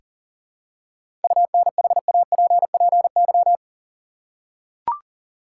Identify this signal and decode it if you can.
Morse code, heard as a keyed tone, about 40 wpm, 700 Hz. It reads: UNHAPPY
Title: Seven Letter Words - 40wpm